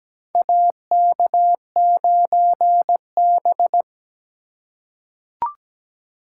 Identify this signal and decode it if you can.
Morse code, heard as a keyed tone, about 17 wpm, 700 Hz. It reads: AK9B
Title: USA Call Signs - 4 Characters - 6x Character Spacing - 17wpm